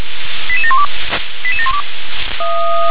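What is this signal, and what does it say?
Five-tone selective call (ZVEI), heard as quick runs of five tones, tones of about 70 ms. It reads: 88712 88712
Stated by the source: ZVEI five-tone selective call 88692, received with an SDR.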